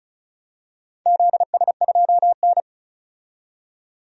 Morse code, heard as a keyed tone, about 36 wpm, 700 Hz. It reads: ZS2N
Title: DXCC-100 Call Signs - 4 Characters - 35wpm